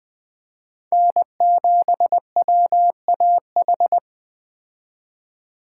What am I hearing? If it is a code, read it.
Morse code, heard as a keyed tone, about 20 wpm, 700 Hz. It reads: N7WAH
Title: Speed Racing - USA 1x3 Call Signs 17wpm